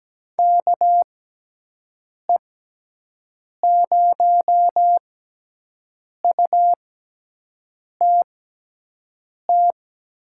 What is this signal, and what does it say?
Morse code, heard as a keyed tone, about 17 wpm, spaced as for 6 wpm, 700 Hz. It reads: KE0UTT
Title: USA Call Signs - 6 Characters - 6x Character Spacing - 17wpm